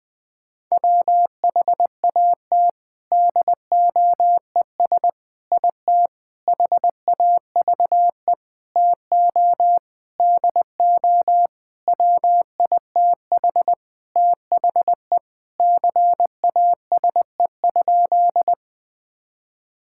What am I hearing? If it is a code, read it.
Morse code, heard as a keyed tone, about 20 wpm, 700 Hz. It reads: WHAT DOES IT HAVE TO DO WITH THE CASE?
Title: Sentences with 4-Word Phrases - 20wpm